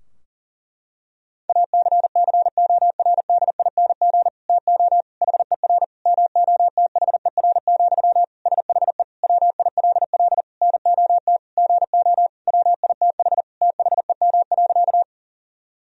Morse code, heard as a keyed tone, about 40 wpm, 700 Hz. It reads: ACCORDING TO HER MOTHER, SHE WILL NOT GO WITH THEM.
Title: Sentences from Top 1000 Words - Review Part 4 - 40wpm